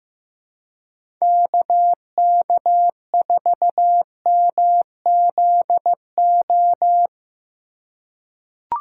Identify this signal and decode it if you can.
Morse code, heard as a keyed tone, about 15 wpm, 700 Hz. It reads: KK4MZO
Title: USA Call Signs - 2x Character Spacing - 15wpm